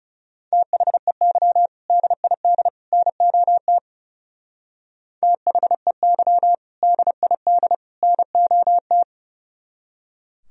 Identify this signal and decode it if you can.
Morse code, heard as a keyed tone, about 37 wpm, 700 Hz. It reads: THEY DID NOT THEY DID NOT
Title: Speed-Racing - Sets of 3 Words 30wpm